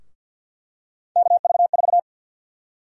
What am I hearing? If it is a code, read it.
Morse code, heard as a keyed tone, about 52 wpm, 700 Hz. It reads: KV4
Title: USA Call Sign Prefixes - 50wpm